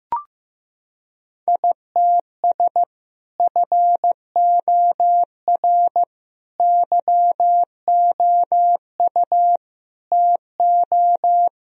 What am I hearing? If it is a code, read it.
Morse code, heard as a keyed tone, about 15 wpm, 700 Hz. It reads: ITS FOR YOU TO
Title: Sentences from Top 1000 Words - Review Part 3 - 15wpm